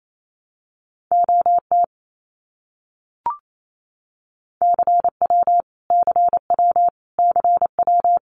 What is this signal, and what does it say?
Morse code, heard as a keyed tone, about 28 wpm, 700 Hz. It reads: OT CW CW CW
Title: Abbreviations Repeated 3x - 28wpm